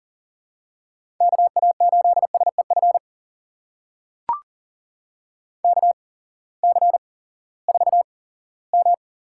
Morse code, heard as a keyed tone, about 40 wpm, spaced as for 20 wpm, 700 Hz. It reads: KA8SEF KC4M
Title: USA Call Signs - 8x Character Spacing - 40wpm